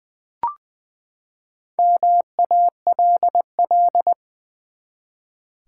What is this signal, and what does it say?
Morse code, heard as a keyed tone, about 20 wpm, 700 Hz. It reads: MALL
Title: Four Letter Words - 20wpm